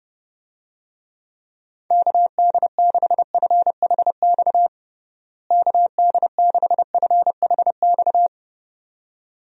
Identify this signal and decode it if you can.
Morse code, heard as a keyed tone, about 30 wpm, 700 Hz. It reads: KD6FHX KD6FHX
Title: USA Call Signs - 6 Characters - Repeated 2x - 30wpm